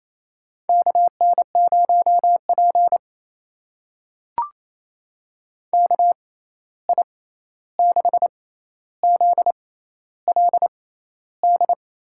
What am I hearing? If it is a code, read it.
Morse code, heard as a keyed tone, about 28 wpm, spaced as for 13 wpm, 700 Hz. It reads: KN0P KI6ZLD
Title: USA Call Signs - 6x Character Spacing - 28wpm